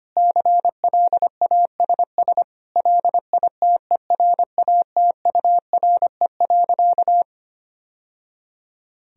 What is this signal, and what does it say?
Morse code, heard as a keyed tone, about 25 wpm, 700 Hz. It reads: CLASS LITERATURE.
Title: Sentences from Top 1000 Words - Review Part 3 - 25wpm